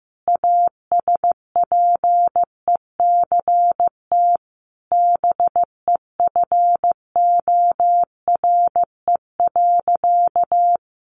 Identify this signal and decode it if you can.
Morse code, heard as a keyed tone, about 15 wpm, 700 Hz. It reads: ASPECT BEFORE.
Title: Sentences from Top 1300 Words 15wpm